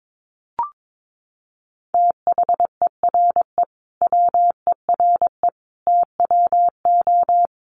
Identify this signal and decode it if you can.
Morse code, heard as a keyed tone, about 22 wpm, 700 Hz. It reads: THERE WERE TWO
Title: Sentences from Top 100 Words 22wpm